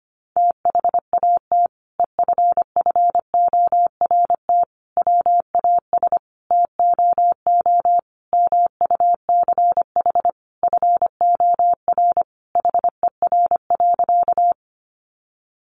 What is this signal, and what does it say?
Morse code, heard as a keyed tone, about 25 wpm, 700 Hz. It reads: THAT EFFORT WAS TOO MUCH FOR HER.
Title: Sentences from Top 1200 Words 25wpm